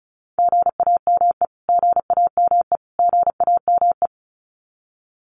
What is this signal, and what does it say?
Morse code, heard as a keyed tone, about 34 wpm, 700 Hz. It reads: GAME GAME GAME
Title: Top 500 Words Repeated 35wpm